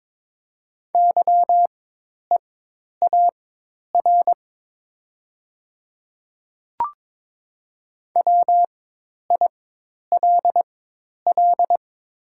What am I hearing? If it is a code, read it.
Morse code, heard as a keyed tone, about 22 wpm, spaced as for 10 wpm, 700 Hz. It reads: YEAR WILL
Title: 4x Character Spacing - Top 500 Words - 22wpm